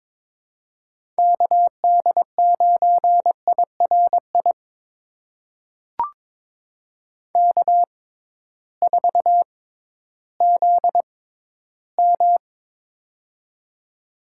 Morse code, heard as a keyed tone, about 22 wpm, spaced as for 13 wpm, 700 Hz. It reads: KD9IRI K4ZM
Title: USA Call Signs - 6x Character Spacing - 22wpm